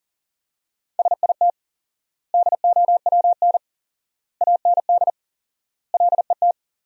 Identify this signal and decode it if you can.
Morse code, heard as a keyed tone, about 40 wpm, 700 Hz. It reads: SIT DOWN AND LET
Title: 4x Word-Spacing - Sentences from Top 500 Words - 40wpm